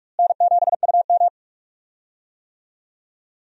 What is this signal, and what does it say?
Morse code, heard as a keyed tone, about 44 wpm, 700 Hz. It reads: N7UM
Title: USA Call Signs - 4 Characters - Repeated 3x - 45wpm